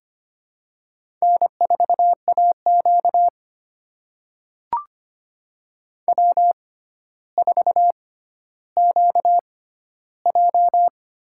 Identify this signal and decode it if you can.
Morse code, heard as a keyed tone, about 25 wpm, spaced as for 13 wpm, 700 Hz. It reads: N4AQ W4QJ
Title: USA Call Signs - 4 Characters - 6x Character Spacing - 25wpm